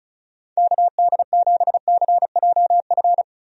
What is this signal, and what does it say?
Morse code, heard as a keyed tone, about 36 wpm, 700 Hz. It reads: KD7CJF
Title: USA Call Signs - 6 Characters - Repeated 3x - 35wpm